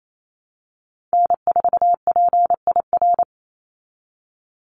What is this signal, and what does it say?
Morse code, heard as a keyed tone, about 28 wpm, 700 Hz. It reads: N4PIR
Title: Call Signs Repeated 3x - 28wpm